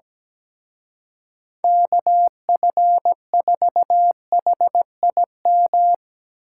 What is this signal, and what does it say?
Morse code, heard as a keyed tone, about 19 wpm, 700 Hz. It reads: KF4HIM
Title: Speed Racing - USA Call Signs 17wpm